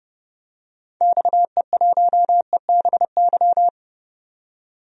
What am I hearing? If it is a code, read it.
Morse code, heard as a keyed tone, about 31 wpm, 700 Hz. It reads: XE1EBY
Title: DXCC-100 Call Signs - 6 Characters - 30wpm